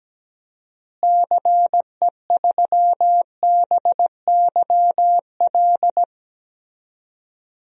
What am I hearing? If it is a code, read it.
Morse code, heard as a keyed tone, about 17 wpm, 700 Hz. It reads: CE3BYL
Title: DXCC-100 Call Signs - 6 Characters - Repeated 3x - 17wpm